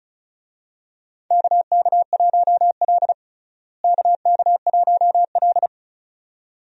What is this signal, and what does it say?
Morse code, heard as a keyed tone, about 34 wpm, 700 Hz. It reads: KK1L KK1L
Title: USA Call Signs - 4 Characters - Repeated 2x - 35wpm